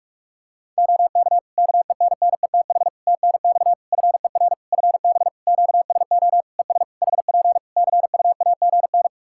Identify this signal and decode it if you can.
Morse code, heard as a keyed tone, about 41 wpm, 700 Hz. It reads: OK KENNETH TNX FER FB QSO ES HP CUAGN
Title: Practice QSO - Exchange 4 - Ending - 45wpm